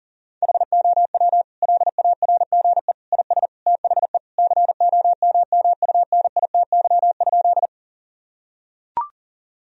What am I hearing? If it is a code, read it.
Morse code, heard as a keyed tone, about 37 wpm, 700 Hz. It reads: HOW LARGE IS THE COMMUNITY?
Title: Sentences from Top 1000 Words - Review Part 4 - 40wpm